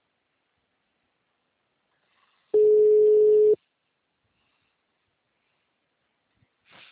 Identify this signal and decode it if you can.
Ringback tone (Europe, 425 Hz).